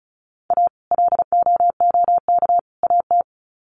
Morse code, heard as a keyed tone, about 34 wpm, 700 Hz. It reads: A LOOK AT
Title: Sets of 3 Words 35wpm